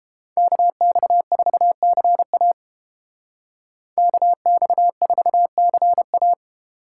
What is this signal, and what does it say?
Morse code, heard as a keyed tone, about 28 wpm, 700 Hz. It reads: KX4CA KX4CA